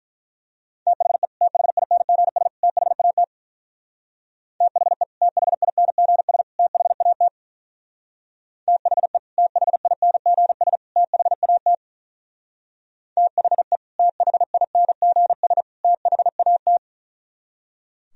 Morse code, heard as a keyed tone, about 43 wpm, 700 Hz. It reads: THE THINGS THAT THE THINGS THAT THE THINGS THAT THE THINGS THAT
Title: Speed-Racing - Sets of 3 Words 35wpm